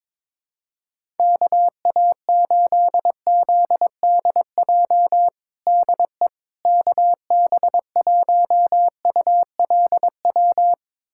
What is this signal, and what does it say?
Morse code, heard as a keyed tone, about 22 wpm, 700 Hz. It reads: KA8ZDJ DE KB1ULW
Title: USA Callsign DE Callsign 22wpm